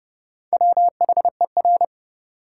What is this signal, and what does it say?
Morse code, heard as a keyed tone, about 32 wpm, 700 Hz. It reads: WHER
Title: Sentences from Top 1500 Words - New Word Drill Alt - 30wpm